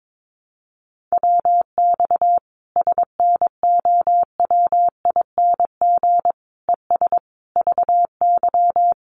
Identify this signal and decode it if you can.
Morse code, heard as a keyed tone, about 22 wpm, 700 Hz. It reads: WX SNOWING ES VY
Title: Common QSO Phrases 22wpm